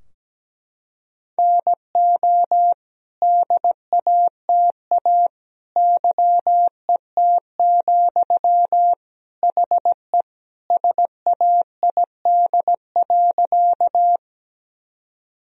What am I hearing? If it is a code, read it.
Morse code, heard as a keyed tone, about 17 wpm, 700 Hz. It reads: NO DATA YET, HE SAID.
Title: Sentences from Top 1000 Words - Review Part 3 - 17wpm